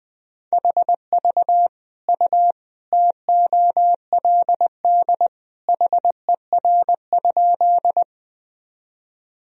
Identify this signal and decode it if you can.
Morse code, heard as a keyed tone, about 19 wpm, 700 Hz. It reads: HV U TOLD HER?
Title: Sentences from Top 400 Words and Abbreviations 20wpm